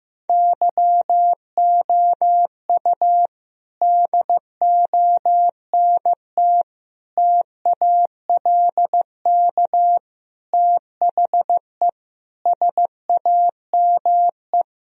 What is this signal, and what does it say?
Morse code, heard as a keyed tone, about 15 wpm, 700 Hz. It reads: YOU DONT TALK THE SAME
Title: Sentences from Top 1000 Words - Review Part 3 - 15wpm